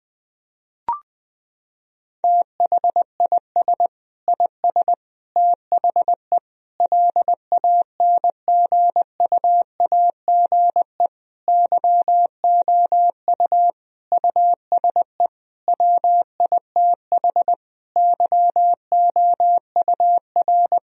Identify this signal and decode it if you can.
Morse code, heard as a keyed tone, about 21 wpm, 700 Hz. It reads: THIS IS THE LANGUAGE YOU USE WITH YOUR
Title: Sentences from Top 1000 Words - Review Part 3 - 20wpm